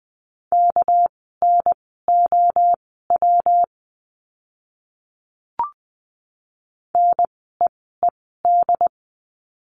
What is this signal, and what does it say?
Morse code, heard as a keyed tone, about 20 wpm, spaced as for 14 wpm, 700 Hz. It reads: KNOW NEED
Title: 230 Lesson 37f - Letter x - Words 20wpm